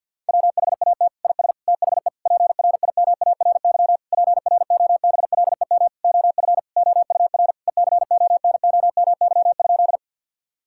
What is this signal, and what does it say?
Morse code, heard as a keyed tone, about 52 wpm, 700 Hz. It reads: WHAT IS THE PRIMARY PROBLEM OF OUR ECONOMY?
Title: Sentences from Top 1000 Words - Review Part 5 - 50wpm